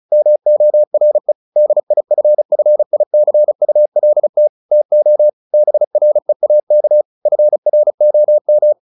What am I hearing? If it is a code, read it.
Morse code, heard as a keyed tone, about 34 wpm, 600 Hz. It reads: MORE DIFFICULT TO BREAK FROM
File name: the old man and the sea / the old man and the sea 1 35wpm